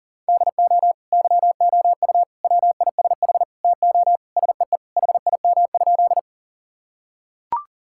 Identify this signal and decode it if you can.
Morse code, heard as a keyed tone, about 40 wpm, 700 Hz. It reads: DO YOU WISH TO SEE HIM?
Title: Sentences from Top 1600 Words 40wpm